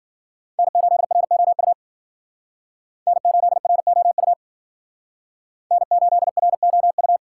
Morse code, heard as a keyed tone, about 53 wpm, 700 Hz. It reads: N8ROU N8ROU N8ROU